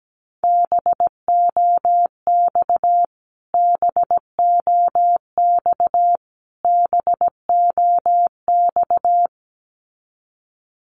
Morse code, heard as a keyed tone, about 17 wpm, 700 Hz. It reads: BOX BOX BOX
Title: Top 500 Words Repeated 17wpm